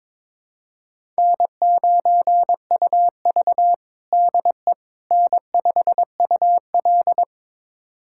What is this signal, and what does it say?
Morse code, heard as a keyed tone, about 22 wpm, 700 Hz. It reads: N9UV DE N5UL
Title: USA Callsign DE Callsign - 4 Characters - 22wpm